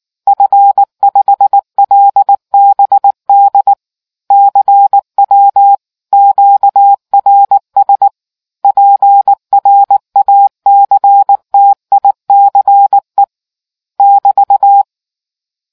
Morse code, heard as a keyed tone, about 19 wpm, 800 Hz. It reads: F5LBD CW QRS PRACTICE <BT>